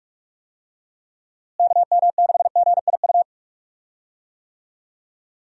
Morse code, heard as a keyed tone, about 45 wpm, 700 Hz. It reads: KM6GIU
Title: USA Call Signs - 6 Characters - Repeated 2x - 45wpm